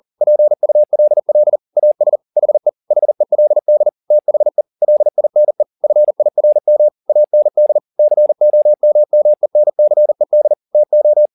Morse code, heard as a keyed tone, about 40 wpm, 600 Hz. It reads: PULL AS HE HELD THE LINE FIRM AND COMMENCED TO